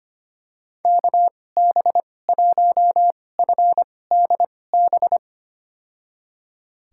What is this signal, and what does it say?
Morse code, heard as a keyed tone, about 25 wpm, spaced as for 18 wpm, 700 Hz. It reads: KB1FDB
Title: USA Call Signs - 2x Character Spacing - 25wpm